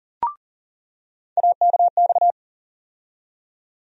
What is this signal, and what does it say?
Morse code, heard as a keyed tone, about 40 wpm, 700 Hz. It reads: AKX
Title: USA Call Sign Postfixes - 3 Letters 40wpm